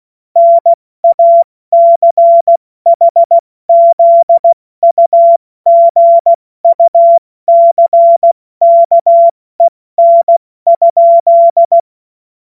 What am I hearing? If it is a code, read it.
Morse code, heard as a keyed tone, about 16 wpm, spaced as for 14 wpm, 680 Hz. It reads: NACHZUGUCKEN?
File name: Morsekurs / Witze / Witz 01. November 2022 70 F80